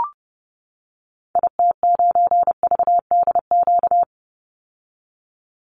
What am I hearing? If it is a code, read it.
Morse code, heard as a keyed tone, about 30 wpm, 700 Hz. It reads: IT9VDQ
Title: Call Signs - 30wpm